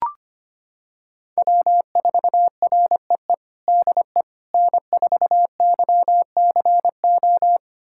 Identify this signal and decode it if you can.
Morse code, heard as a keyed tone, about 25 wpm, 700 Hz. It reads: W4REE DE N4YCO
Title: USA Callsign DE Callsign - 5 Characters - 25wpm